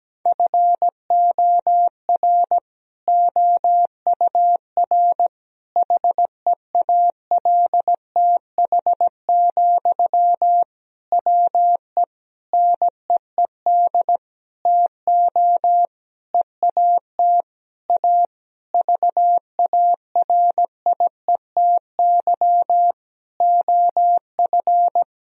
Morse code, heard as a keyed tone, about 17 wpm, 700 Hz. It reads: FOR OUR HEALTH, WE NEED TO EAT A VARIETY OF
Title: Sentences from Top 1000 Words - Review Part 4 - 17wpm